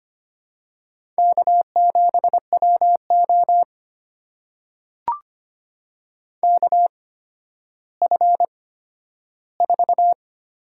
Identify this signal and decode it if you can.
Morse code, heard as a keyed tone, about 25 wpm, spaced as for 12 wpm, 700 Hz. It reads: K7WO KF4
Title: USA Call Signs - 8x Character Spacing - 25wpm